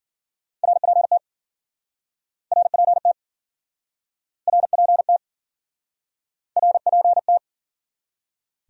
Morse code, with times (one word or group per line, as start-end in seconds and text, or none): 0.6-1.2 RPT
2.5-3.1 RPT
4.5-5.2 RPT
6.6-7.4 RPT